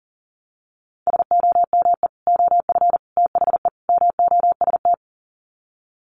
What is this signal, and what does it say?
Morse code, heard as a keyed tone, about 40 wpm, 700 Hz. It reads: SOME OF THE MOST
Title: Sets of 4 Words 40wpm